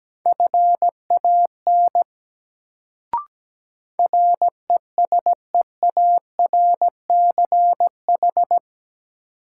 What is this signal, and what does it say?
Morse code, heard as a keyed tone, about 17 wpm, 700 Hz. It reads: FAN RESEARCH
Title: Sentences from Top 1000 Words - Review Part 4 - New Word Drill - 17wpm